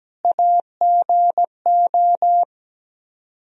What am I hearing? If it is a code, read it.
Morse code, heard as a keyed tone, about 17 wpm, 700 Hz. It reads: AGO
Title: Speed-Racing - Sets of 4 Words 17wpm